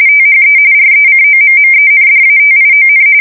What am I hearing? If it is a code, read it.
RTTY, about 45.45 baud. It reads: THE RECOMMENDATION
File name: fsk170 / usnavy3